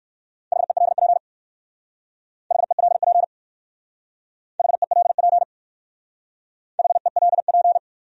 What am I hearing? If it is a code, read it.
Morse code, heard as a keyed tone, about 54 wpm, 700 Hz. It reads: HELP HELP HELP HELP
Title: Speed-Racing Top 300 Words 45wpm